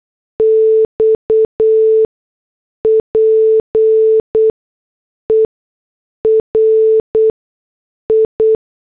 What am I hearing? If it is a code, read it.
Morse code, heard as a keyed tone, about 8 wpm, spaced as for 6 wpm, 440 Hz. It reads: XPERI